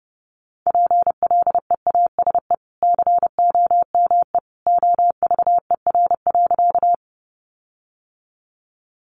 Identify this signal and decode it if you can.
Morse code, heard as a keyed tone, about 30 wpm, 700 Hz. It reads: PLEASE COME OVER.
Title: Sentences from Top 1500 Words 30wpm